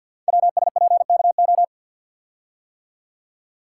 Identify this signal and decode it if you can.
Morse code, heard as a keyed tone, about 50 wpm, 700 Hz. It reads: WSPKO
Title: Code Groups - 5 Letters - 50wpm